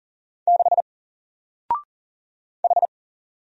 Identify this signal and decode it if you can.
Morse code, heard as a keyed tone, about 40 wpm, 700 Hz. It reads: 6 H
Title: Single Character (Letter:Number:Punctuation:Prosign) 40wpm